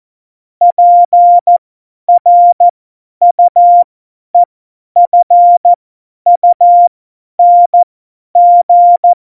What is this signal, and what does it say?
Morse code, heard as a keyed tone, about 14 wpm, spaced as for 10 wpm, 700 Hz. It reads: PRUEFUNG